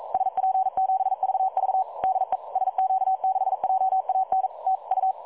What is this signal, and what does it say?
Morse code, heard as a keyed tone, about 42 wpm, 760 Hz. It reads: SP7VV DE SQ6JAN TU
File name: dlazaawansowanych / nagraniazpasma / websdr recording 3565.0kHz